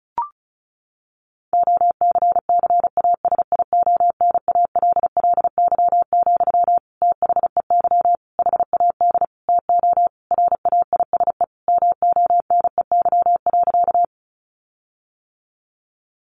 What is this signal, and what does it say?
Morse code, heard as a keyed tone, about 34 wpm, 700 Hz. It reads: OCCASIONALLY, THEY HAD TO RAISE MONEY.
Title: Sentences from Top 1400 Words 35wpm